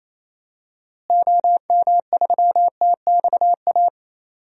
Morse code, heard as a keyed tone, about 28 wpm, 700 Hz. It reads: OM3TXA
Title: DXCC-100 Call Signs - 6 Characters - Repeated 2x - 28wpm